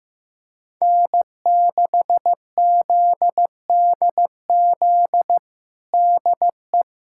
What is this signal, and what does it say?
Morse code, heard as a keyed tone, about 15 wpm, 700 Hz. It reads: N6ZDZ DE
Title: USA Callsign DE Callsign - 5 Characters - 15wpm